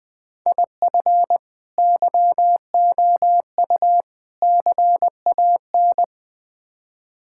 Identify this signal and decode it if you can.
Morse code, heard as a keyed tone, about 20 wpm, 700 Hz. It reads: IF YOU CAN
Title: Build A Phrase - 4-Word Phrases - 20wpm